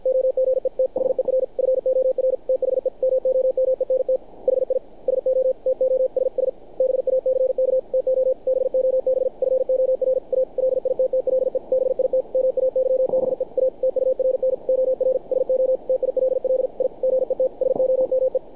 Morse code, heard as a keyed tone, about 49 wpm, 530 Hz. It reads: OGETHER FOR THE MOMENT HI SO TOSS BACK TO BOB FOR A LITTLE BIT MAYBE A TURN OR SO TILL I GET SOME
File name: archived / CRSnet